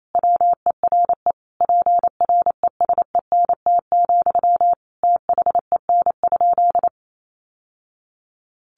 Morse code, heard as a keyed tone, about 28 wpm, 700 Hz. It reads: WERE PRESENT, THEN?
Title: Sentences from Top 800 Words 28wpm